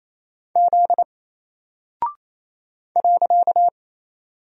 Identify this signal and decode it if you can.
Morse code, heard as a keyed tone, about 28 wpm, 700 Hz. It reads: Z .
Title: Single Character (Letter:Number:Punctuation:Prosign) 28wpm